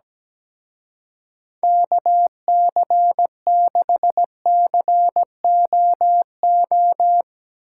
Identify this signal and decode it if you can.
Morse code, heard as a keyed tone, about 17 wpm, 700 Hz. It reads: KC6COO
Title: Speed Racing - USA 2x3 Call Signs 17wpm